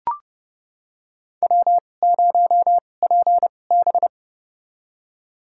Morse code, heard as a keyed tone, about 30 wpm, spaced as for 22 wpm, 700 Hz. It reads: W0PB
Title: USA Call Signs - 2x Character Spacing - 30wpm